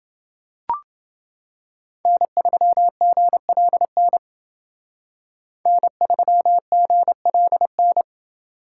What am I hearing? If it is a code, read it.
Morse code, heard as a keyed tone, about 28 wpm, 700 Hz. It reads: N3GLN N3GLN